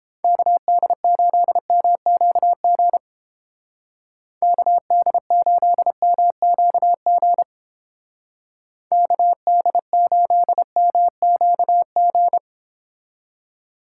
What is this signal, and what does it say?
Morse code, heard as a keyed tone, about 30 wpm, 700 Hz. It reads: KD8MQG KD8MQG KD8MQG